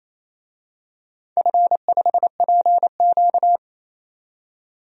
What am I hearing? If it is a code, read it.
Morse code, heard as a keyed tone, about 27 wpm, 700 Hz. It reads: F5PQ
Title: DXCC-100 Call Signs - 4 Characters - 28wpm